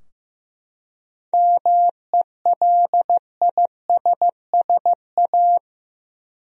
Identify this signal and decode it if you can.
Morse code, heard as a keyed tone, about 16 wpm, 700 Hz. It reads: MELISSA
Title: USA Common Names - 15wpm